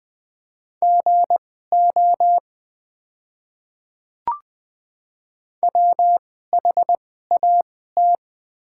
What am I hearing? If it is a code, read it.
Morse code, heard as a keyed tone, about 20 wpm, spaced as for 14 wpm, 700 Hz. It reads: GO WHAT
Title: 2x Character Spacing - Top 100 Words - 20wpm